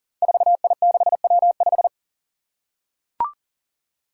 Morse code, heard as a keyed tone, about 37 wpm, 700 Hz. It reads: 4I6W5
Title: Five Challenging Letter-Number Combinations 40wpm